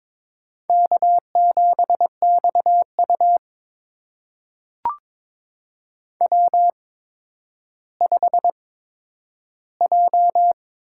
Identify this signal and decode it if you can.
Morse code, heard as a keyed tone, about 22 wpm, spaced as for 11 wpm, 700 Hz. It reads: K7XU W5J
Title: USA Call Signs - 4 Characters - 8x Character Spacing - 22wpm